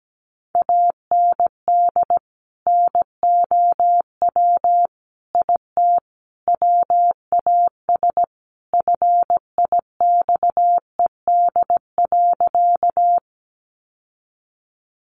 Morse code, heard as a keyed tone, about 17 wpm, 700 Hz. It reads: AND NOW IT WAS FIXED.
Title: Sentences from Top 1100 Words 17wpm